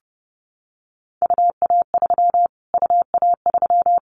Morse code, heard as a keyed tone, about 30 wpm, 700 Hz. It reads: UA3 UA3
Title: Callsign Prefixes for Top 25 DXCC - Repeated 3x - 30wpm